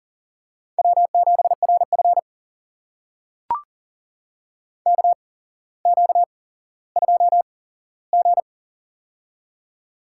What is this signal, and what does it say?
Morse code, heard as a keyed tone, about 39 wpm, spaced as for 17 wpm, 700 Hz. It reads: W7RF KQ2G
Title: USA Call Signs - 8x Character Spacing - 40wpm